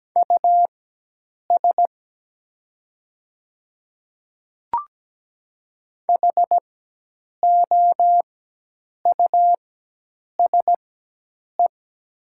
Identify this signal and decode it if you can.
Morse code, heard as a keyed tone, about 17 wpm, spaced as for 8 wpm, 700 Hz. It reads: US HOUSE
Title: 4x Character Spacing - Top 200 Words - 17wpm